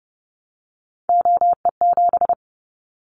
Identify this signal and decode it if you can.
Morse code, heard as a keyed tone, about 30 wpm, 700 Hz. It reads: OE7
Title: Callsign Prefixes for Top 100 DXCC - 30wpm